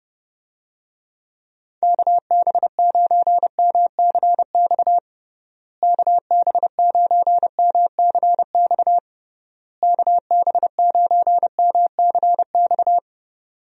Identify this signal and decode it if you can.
Morse code, heard as a keyed tone, about 30 wpm, 700 Hz. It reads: KB9MCX KB9MCX KB9MCX
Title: USA Call Signs - 6 Characters - Repeated 3x - 30wpm